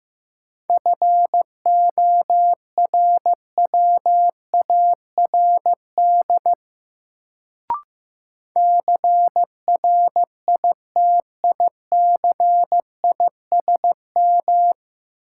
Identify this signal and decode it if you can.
Morse code, heard as a keyed tone, about 15 wpm, 700 Hz. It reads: FORWARD CRITICISM
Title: Sentences from Top 1300 Words - New Word Drill - 15wpm